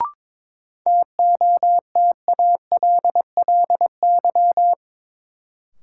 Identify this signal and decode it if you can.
Morse code, heard as a keyed tone, about 21 wpm, 700 Hz. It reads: TOTALLY
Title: Sentences from Top 1600 Words - New Word Drill Alt - 22wpm